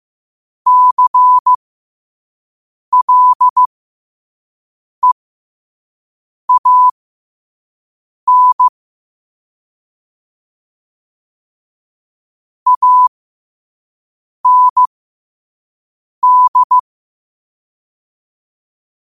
Morse code, heard as a keyed tone, about 15 wpm, spaced as for 5 wpm, 1000 Hz. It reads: CLEAN AND
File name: qotdfiles / 5wpm